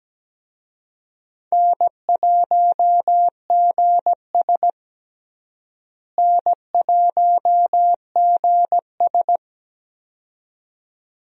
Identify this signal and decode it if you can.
Morse code, heard as a keyed tone, about 17 wpm, 700 Hz. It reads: N1GS N1GS
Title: USA Call Signs - 4 Characters - Repeated 2x - 17wpm